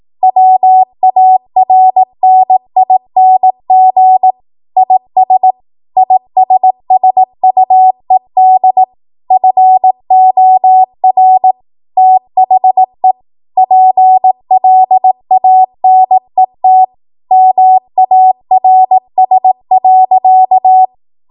Morse code, heard as a keyed tone, about 18 wpm, 750 Hz. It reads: WARNING IS ISSUED FOR THE PLANET MARS.
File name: Morse / 18WPM